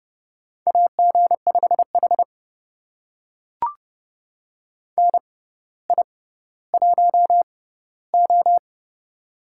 Morse code, heard as a keyed tone, about 32 wpm, spaced as for 16 wpm, 700 Hz. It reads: AG5H NI1O
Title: USA Call Signs - 4 Characters - 6x Character Spacing - 30wpm